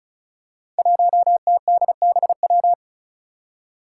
Morse code, heard as a keyed tone, about 36 wpm, 700 Hz. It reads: 1TDBW
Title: Five Challenging Letter-Number Combinations 35wpm